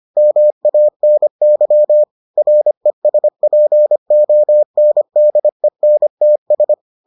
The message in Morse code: MANY RESPONDENTS

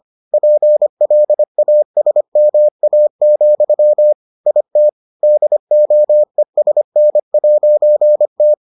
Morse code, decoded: PLASMA, IT DOESN'T

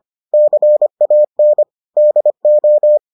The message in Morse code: CAN DO

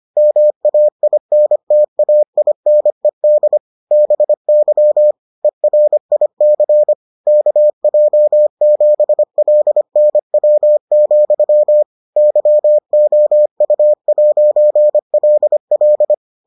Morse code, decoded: MAINTAINED BY ERIC KJ7LNW, YOU'LL

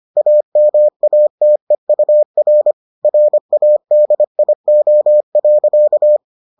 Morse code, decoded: AMATEUR RADIO.

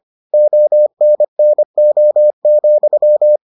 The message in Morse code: ONNO,

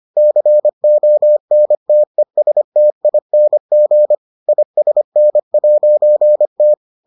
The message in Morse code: CONTESTING ISN'T

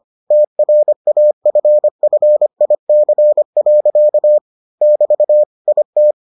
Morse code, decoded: TRAFFIC. <BT> IT

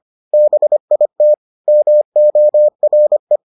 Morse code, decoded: BIT MORE

